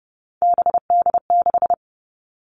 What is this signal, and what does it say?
Morse code, heard as a keyed tone, about 30 wpm, 700 Hz. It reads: BD6
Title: All Cut Numbers - 3-Digit 30wpm